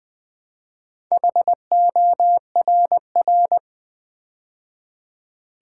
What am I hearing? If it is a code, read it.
Morse code, heard as a keyed tone, about 20 wpm, 700 Hz. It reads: HORR
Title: Build A Word - 2 Letters At A Time - 6-Letter Words - 20wpm